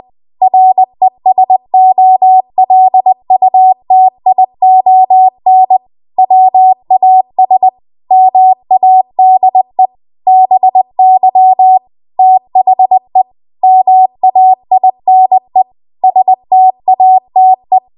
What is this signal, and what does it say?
Morse code, heard as a keyed tone, about 20 wpm, 750 Hz. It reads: RESOLUTION WAS MADE BY THE MAINE STATE
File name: Morse / 20WPM